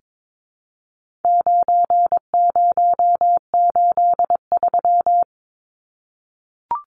Morse code, heard as a keyed tone, about 22 wpm, 700 Hz. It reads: 9083
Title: Numbers - 4-Digit 22wpm